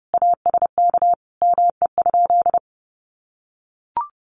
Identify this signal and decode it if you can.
Morse code, heard as a keyed tone, about 30 wpm, 700 Hz. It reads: ASK ME?
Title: Sentences from Top 200 Words 30wpm